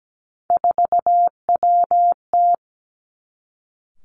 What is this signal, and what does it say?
Morse code, heard as a keyed tone, about 17 wpm, 700 Hz. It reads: 4WT
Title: Three Challenging Letter-Number Combinations 17wpm